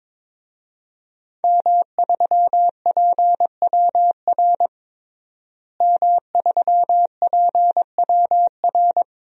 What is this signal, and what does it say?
Morse code, heard as a keyed tone, about 21 wpm, 700 Hz. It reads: M3PWR M3PWR
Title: DXCC-100 Call Signs - 5 Characters - Repeated 2x - 22wpm